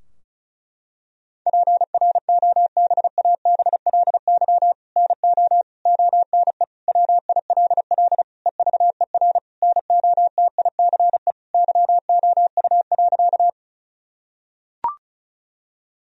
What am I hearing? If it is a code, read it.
Morse code, heard as a keyed tone, about 36 wpm, 700 Hz. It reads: PROBABLY NO ONE WILL EVER NOTICE YOU.
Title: Sentences from Top 1000 Words - Review Part 4 - 35wpm